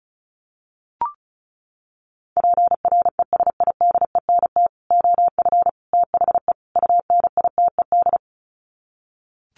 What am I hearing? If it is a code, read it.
Morse code, heard as a keyed tone, about 34 wpm, 700 Hz. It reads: PRESIDENT OF THE UNITED
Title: Sets of 4 Words 35wpm